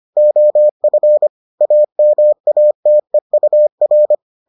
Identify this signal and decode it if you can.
Morse code, heard as a keyed tone, about 25 wpm, 600 Hz. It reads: OF AMATEUR